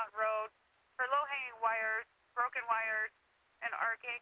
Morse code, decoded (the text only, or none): none